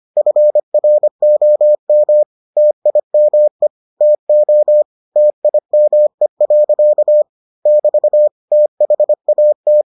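Morse code, heard as a keyed tone, about 25 wpm, 600 Hz. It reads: FROM TIME TO TIME. <BT> THAT
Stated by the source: Morse edition of the Foundations of Amateur Radio episodes.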